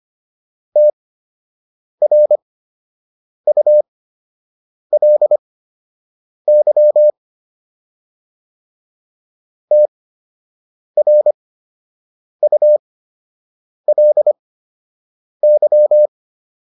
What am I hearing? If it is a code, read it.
Morse code, heard as a keyed tone, about 25 wpm, spaced as for 7 wpm, 600 Hz. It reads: TRULY TRULY